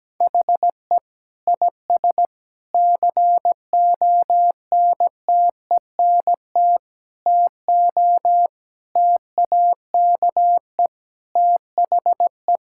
Morse code, heard as a keyed tone, about 17 wpm, 700 Hz. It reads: HE IS CONTENT TO TAKE THE
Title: Sentences from Top 1000 Words - Review Part 6 - 17wpm